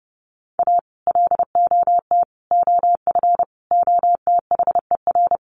Morse code, heard as a keyed tone, about 30 wpm, 700 Hz. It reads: A LOT OF OTHER
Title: Sets of 5 Words Encore2 30wpm